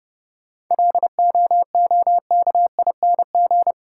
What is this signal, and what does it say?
Morse code, heard as a keyed tone, about 30 wpm, 700 Hz. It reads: LOOKING
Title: Sentences from Top 1000 Words - Review Part 6 - New Word Drill - 30wpm